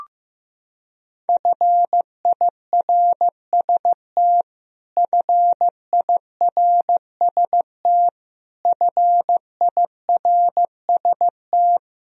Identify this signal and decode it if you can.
Morse code, heard as a keyed tone, about 15 wpm, 700 Hz. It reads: FIRST FIRST FIRST
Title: Top 100 Words Repeated 15wpm